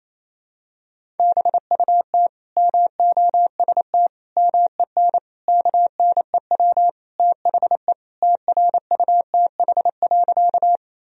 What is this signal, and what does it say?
Morse code, heard as a keyed tone, about 28 wpm, 700 Hz. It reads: BUT MOST MEN KNEW THE TRUTH.
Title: Sentences from Top 1000 Words - Review Part 1 - 28wpm